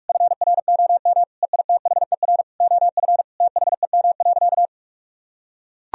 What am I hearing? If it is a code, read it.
Morse code, heard as a keyed tone, about 41 wpm, 700 Hz. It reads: FROM EITHER OF THEM.
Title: Sentences from Top 1000 Words - Review Part 4 - 45wpm